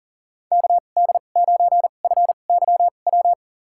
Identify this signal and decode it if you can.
Morse code, heard as a keyed tone, about 39 wpm, spaced as for 28 wpm, 700 Hz. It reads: KD9FYW
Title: USA Call Signs - 6 Characters - 2x Character Spacing - 40wpm